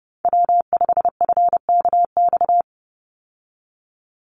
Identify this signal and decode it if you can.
Morse code, heard as a keyed tone, about 30 wpm, 700 Hz. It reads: W5FKX
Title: Call Signs Repeated 3x - 30wpm